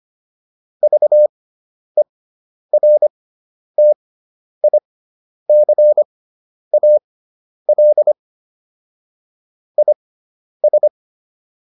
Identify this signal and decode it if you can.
Morse code, heard as a keyed tone, about 25 wpm, spaced as for 10 wpm, 600 Hz. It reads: VERTICAL IS